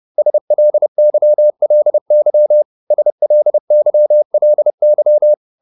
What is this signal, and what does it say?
Morse code, heard as a keyed tone, about 30 wpm, 600 Hz. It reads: SLYLY SLYLY